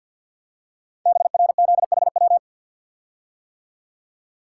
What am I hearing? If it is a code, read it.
Morse code, heard as a keyed tone, about 51 wpm, 700 Hz. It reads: BRZHW
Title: Code Groups - 5 Letters - 50wpm